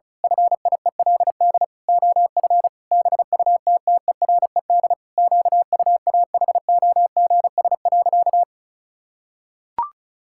Morse code, heard as a keyed tone, about 34 wpm, 700 Hz. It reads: FIELD OF BUTTERED QUAHOGS.
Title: Pangram Challenge v2 - 35wpm